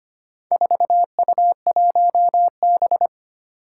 Morse code, heard as a keyed tone, about 25 wpm, 700 Hz. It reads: 4U1B
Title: Four Challenging Letter-Number Combinations 25wpm